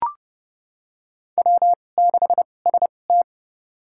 Morse code, heard as a keyed tone, about 30 wpm, spaced as for 22 wpm, 700 Hz. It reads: W6ST
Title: USA Call Signs - 2x Character Spacing - 30wpm